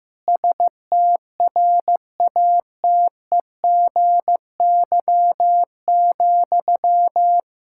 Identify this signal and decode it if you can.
Morse code, heard as a keyed tone, about 15 wpm, 700 Hz. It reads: STRATEGY,
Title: Sentences from Top 1000 Words - Review Part 5 - 15wpm